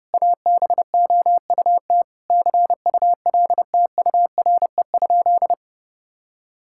Morse code, heard as a keyed tone, about 30 wpm, 700 Hz. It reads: ABOUT CULTURE?
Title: Sentences from Top 1000 Words - Review Part 6 - 30wpm